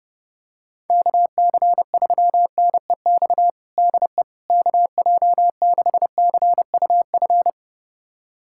Morse code, heard as a keyed tone, about 30 wpm, 700 Hz. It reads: KC3NEX DE KJ6CUF
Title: USA Callsign DE Callsign - 6 Characters - 30wpm